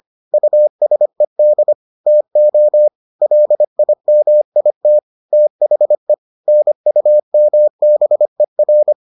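Morse code, heard as a keyed tone, about 25 wpm, 600 Hz. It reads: USED TO LIMIT THE NUMBER
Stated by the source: Morse edition of the Foundations of Amateur Radio episodes.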